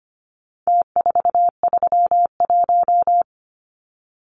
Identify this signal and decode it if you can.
Morse code, heard as a keyed tone, about 25 wpm, 700 Hz. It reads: T431
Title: Numbers - Contest Style - 4-Digit 25wpm